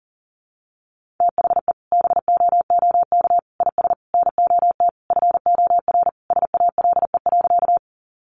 Morse code, heard as a keyed tone, about 40 wpm, 700 Hz. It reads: THE BOOK IS NOT FOR SALE.
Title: Sentences from Top 1400 Words 40wpm